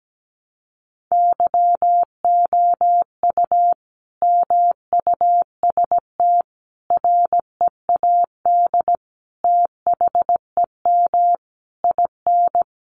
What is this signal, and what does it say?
Morse code, heard as a keyed tone, about 17 wpm, 700 Hz. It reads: YOU MUST READ THEM IN
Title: Sentences from Top 900 Words 17wpm